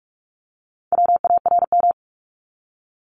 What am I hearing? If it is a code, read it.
Morse code, heard as a keyed tone, about 46 wpm, 700 Hz. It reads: WARM
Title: Top 500 Words 45wpm